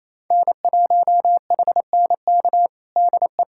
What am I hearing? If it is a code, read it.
Morse code, heard as a keyed tone, about 28 wpm, 700 Hz. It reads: N1HNK DE
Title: USA Callsign DE Callsign - 5 Characters - 28wpm